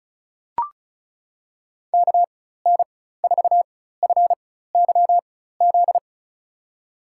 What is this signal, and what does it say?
Morse code, heard as a keyed tone, about 36 wpm, spaced as for 16 wpm, 700 Hz. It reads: KN4FYZ
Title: USA Call Signs - 4x Character Spacing - 35wpm